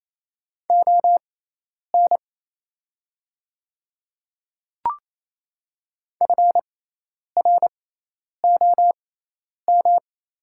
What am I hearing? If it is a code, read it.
Morse code, heard as a keyed tone, about 27 wpm, spaced as for 10 wpm, 700 Hz. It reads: ON FROM
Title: 6x Character Spacing - Top 200 Words - 28wpm